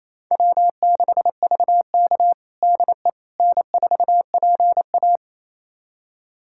Morse code, heard as a keyed tone, about 28 wpm, 700 Hz. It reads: W6VK DE N4PA
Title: USA Callsign DE Callsign - 4 Characters - 28wpm